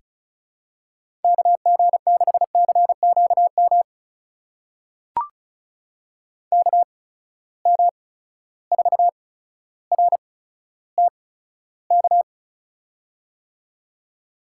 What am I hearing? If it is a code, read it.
Morse code, heard as a keyed tone, about 34 wpm, spaced as for 15 wpm, 700 Hz. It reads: KG6CQM KM4RTK
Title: USA Call Signs - 6 Characters - 8x Character Spacing - 35wpm